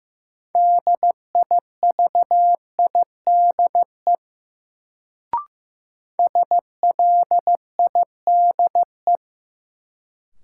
Morse code, heard as a keyed tone, about 15 wpm, 700 Hz. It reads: DIVIDE SLIDE
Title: Sentences from Top 1600 Words - New Word Drill - 15wpm